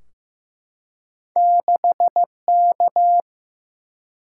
Morse code, heard as a keyed tone, about 15 wpm, 700 Hz. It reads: 6K
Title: Single-Letter Single-Number 15wpm